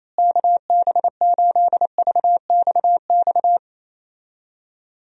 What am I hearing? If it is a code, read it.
Morse code, heard as a keyed tone, about 28 wpm, 700 Hz. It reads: KB8VXX
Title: USA Call Signs - 6 Characters - Repeated 2x - 28wpm